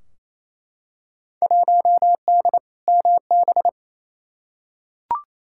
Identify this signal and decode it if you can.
Morse code, heard as a keyed tone, about 28 wpm, 700 Hz. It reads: 1D MB
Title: Field Day Exchanges Encore - 28wpm